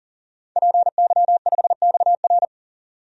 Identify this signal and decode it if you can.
Morse code, heard as a keyed tone, about 37 wpm, 700 Hz. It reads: PY5XR
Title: DXCC-100 Call Signs - 5 Characters - 40wpm